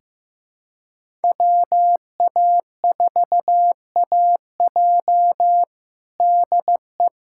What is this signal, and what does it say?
Morse code, heard as a keyed tone, about 15 wpm, 700 Hz. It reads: WA4AJ DE
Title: USA Callsign DE Callsign 15wpm